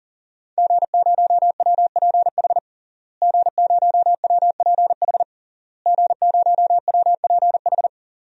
Morse code, heard as a keyed tone, about 40 wpm, 700 Hz. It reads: G0WPH G0WPH G0WPH
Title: DXCC-100 Call Signs - 5 Characters - Repeated 3x - 40wpm